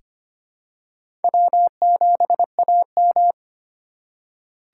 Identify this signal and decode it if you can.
Morse code, heard as a keyed tone, about 25 wpm, 700 Hz. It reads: W7AM
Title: USA Call Signs - 4 Characters - 2x Character Spacing - 25wpm